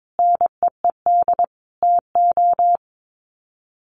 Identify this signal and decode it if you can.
Morse code, heard as a keyed tone, about 22 wpm, 700 Hz. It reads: NEED TO
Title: Sets of 5 Words 22wpm